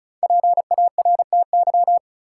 Morse code, heard as a keyed tone, about 36 wpm, 700 Hz. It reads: PARTY
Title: Sentences from Top 1400 Words - New Word Drill - 35wpm